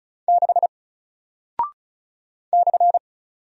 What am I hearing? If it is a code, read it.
Morse code, heard as a keyed tone, about 36 wpm, 700 Hz. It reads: 6 /
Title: Single Character (Letter:Number:Punctuation:Prosign) 35wpm